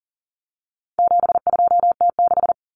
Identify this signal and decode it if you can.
Morse code, heard as a keyed tone, about 40 wpm, 700 Hz. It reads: 72T6
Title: Common Cut Numbers - 4-Digit 40wpm